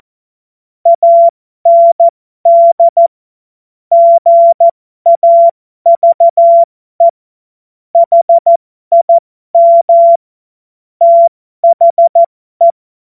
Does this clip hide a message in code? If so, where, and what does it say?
Yes: Morse code from 0.9 to 12.7 s, heard as a keyed tone, about 14 wpm, spaced as for 12 wpm, 680 Hz. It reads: AND GAVE HIM THE